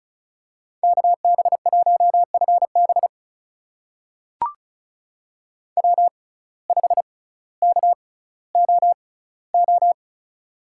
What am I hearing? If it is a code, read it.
Morse code, heard as a keyed tone, about 36 wpm, spaced as for 18 wpm, 700 Hz. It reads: KB1FB W5KOO
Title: USA Call Signs - 5 Characters - 6x Character Spacing - 35wpm